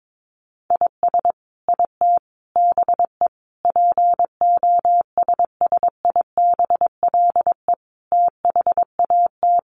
Morse code, heard as a keyed tone, about 22 wpm, 700 Hz. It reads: IS IT BE POSSIBLE THAT
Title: Sentences from Top 1200 Words 22wpm